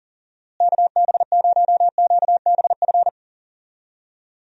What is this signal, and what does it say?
Morse code, heard as a keyed tone, about 39 wpm, 700 Hz. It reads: KB0QBF
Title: USA Call Signs - 6 Characters - Repeated 2x - 40wpm